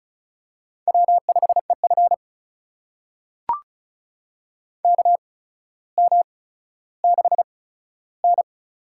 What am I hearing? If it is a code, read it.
Morse code, heard as a keyed tone, about 36 wpm, spaced as for 15 wpm, 700 Hz. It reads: W5EF KM6N
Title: USA Call Signs - 4 Characters - 8x Character Spacing - 35wpm